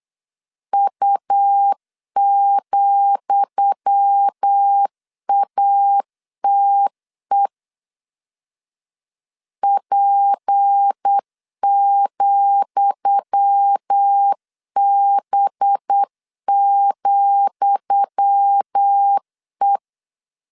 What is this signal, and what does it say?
Morse code, heard as a keyed tone, about 8 wpm, 800 Hz. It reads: U,ATE P,B,E
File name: LEZIONI / 10 SIGNS / 10D COMA 8wpm 3749